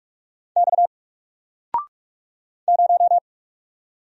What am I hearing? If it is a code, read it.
Morse code, heard as a keyed tone, about 44 wpm, 700 Hz. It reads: X 0
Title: Single Letter-Number 45wpm